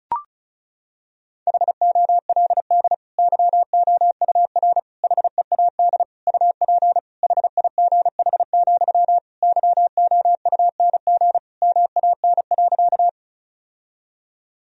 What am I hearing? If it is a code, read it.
Morse code, heard as a keyed tone, about 34 wpm, 700 Hz. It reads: HOLD YOUR HEAD UP HIGH, YOUNG MAN.
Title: Sentences from Top 1000 Words - Review Part 2 - 35wpm